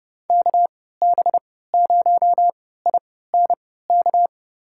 Morse code, heard as a keyed tone, about 29 wpm, spaced as for 17 wpm, 700 Hz. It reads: KB0INK
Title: USA Call Signs - 3x Character Spacing - 30wpm